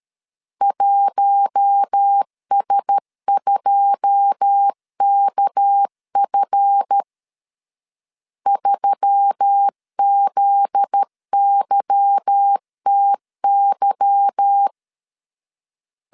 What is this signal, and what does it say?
Morse code, heard as a keyed tone, about 13 wpm, 800 Hz. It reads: 1S2KF 3ZYTY
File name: LEZIONI / 10 SIGNS / 10D COMA 12wpm 2544